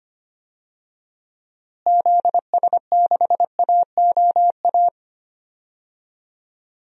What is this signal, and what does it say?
Morse code, heard as a keyed tone, about 25 wpm, 700 Hz. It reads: ZS6AOA
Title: DXCC-100 Call Signs - 6 Characters - Repeated 3x - 25wpm